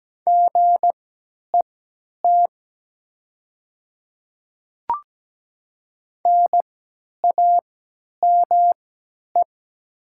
Morse code, heard as a keyed tone, about 17 wpm, spaced as for 10 wpm, 700 Hz. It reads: GET NAME
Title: 3x Character Spacing - Top 300 Words - 17wpm